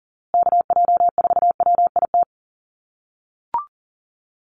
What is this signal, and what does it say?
Morse code, heard as a keyed tone, about 40 wpm, 700 Hz. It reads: KJ4WIT
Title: USA Call Signs Repeated 3x 40wpm